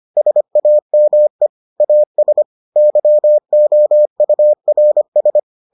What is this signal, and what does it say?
Morse code, heard as a keyed tone, about 25 wpm, 600 Hz. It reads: SAME AS YOURS